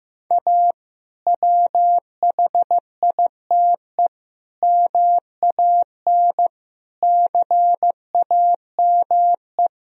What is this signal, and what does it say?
Morse code, heard as a keyed tone, about 15 wpm, 700 Hz. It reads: A WHITE MAN CAME
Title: Sentences from Top 1000 Words - Review Part 1 - 15wpm